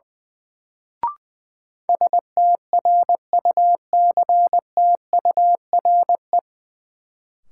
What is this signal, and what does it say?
Morse code, heard as a keyed tone, about 20 wpm, 700 Hz. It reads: STRUCTURE
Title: Sentences from Top 1000 Words - Review Part 6 - New Word Drill - 20wpm